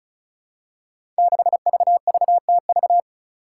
Morse code, heard as a keyed tone, about 36 wpm, 700 Hz. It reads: 6VVTV
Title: Five Challenging Letter-Number Combinations 35wpm